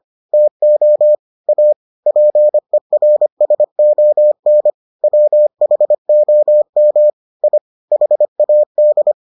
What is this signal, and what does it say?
Morse code, heard as a keyed tone, about 25 wpm, 600 Hz. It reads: TO A PERSON WHOM I HAD